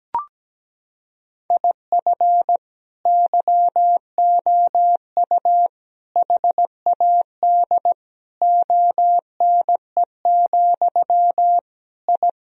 Morse code, heard as a keyed tone, about 17 wpm, 700 Hz. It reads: IF YOU HAD ONE, I
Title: Sentences from Top 1000 Words - Review Part 3 - 17wpm